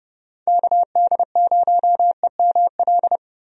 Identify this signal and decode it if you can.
Morse code, heard as a keyed tone, about 30 wpm, 700 Hz. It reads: KD0EML
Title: USA Call Signs - 6 Characters - 30wpm